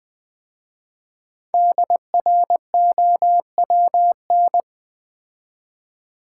Morse code, heard as a keyed tone, about 20 wpm, 700 Hz. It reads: DROWN
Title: 5 Letter Words - First 3 Letters Given - 20wpm